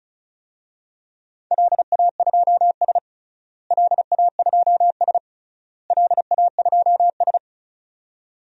Morse code, heard as a keyed tone, about 34 wpm, 700 Hz. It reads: LA2S LA2S LA2S
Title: DXCC-100 Call Signs - 4 Characters - Repeated 3x - 35wpm